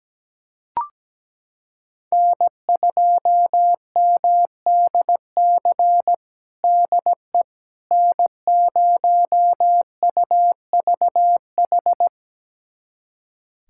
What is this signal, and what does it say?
Morse code, heard as a keyed tone, about 17 wpm, 700 Hz. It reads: N2MDC DE N0UVH
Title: USA Callsign DE Callsign - 5 Characters - 17wpm